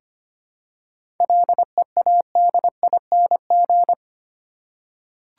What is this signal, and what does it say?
Morse code, heard as a keyed tone, about 25 wpm, 700 Hz. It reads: LEADING
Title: Seven Letter Words - 25wpm